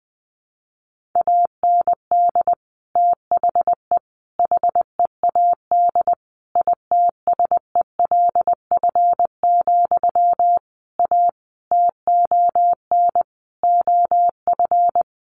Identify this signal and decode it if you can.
Morse code, heard as a keyed tone, about 20 wpm, 700 Hz. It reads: AND THE HEAD ITSELF, A TON OF
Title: A Sound Of Thunder 20wpm